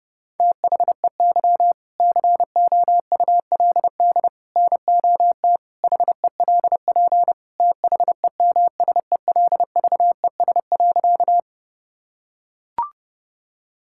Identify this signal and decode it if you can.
Morse code, heard as a keyed tone, about 30 wpm, 700 Hz. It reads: THEY COULD NOT HELP THEMSELVES.
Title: Sentences from Top 1000 Words - Review Part 5 - 30wpm